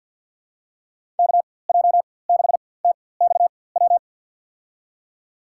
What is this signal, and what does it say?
Morse code, heard as a keyed tone, about 52 wpm, spaced as for 23 wpm, 700 Hz. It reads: KJ6TXW
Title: USA Call Signs - 6 Characters - 4x Character Spacing - 50wpm